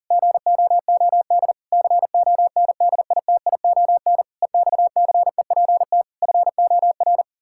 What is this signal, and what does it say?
Morse code, heard as a keyed tone, about 40 wpm, 700 Hz. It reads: GOOD CONDITION EXCEPT FOR
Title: Pangram Challenge v2 - 40wpm